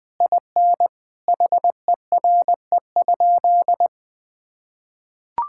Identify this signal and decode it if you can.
Morse code, heard as a keyed tone, about 20 wpm, 700 Hz. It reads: IN HERE?
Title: Sentences from Top 1000 Words - Review Part 6 - 20wpm